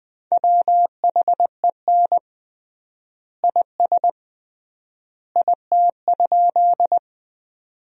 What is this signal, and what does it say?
Morse code, heard as a keyed tone, about 20 wpm, 700 Hz. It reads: WHEN IS IT?
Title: 3x Word-Spacing - Sentences from Top 100 Words - 20wpm